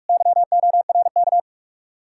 Morse code, heard as a keyed tone, about 59 wpm, 700 Hz. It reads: YORK